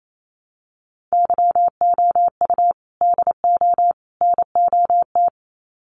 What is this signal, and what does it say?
Morse code, heard as a keyed tone, about 28 wpm, 700 Hz. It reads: YOU DO NOT
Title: Sets of 3 Words 28wpm